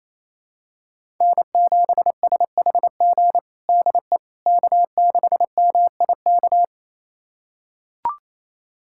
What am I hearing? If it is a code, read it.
Morse code, heard as a keyed tone, about 28 wpm, 700 Hz. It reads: N7SHG DE K6MIK
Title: USA Callsign DE Callsign - 5 Characters - 28wpm